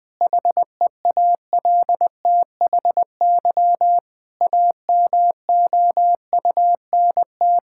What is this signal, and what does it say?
Morse code, heard as a keyed tone, about 21 wpm, 700 Hz. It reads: HEALTHY AMOUNT
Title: Sentences from Top 1000 Words - Review Part 5 - 20wpm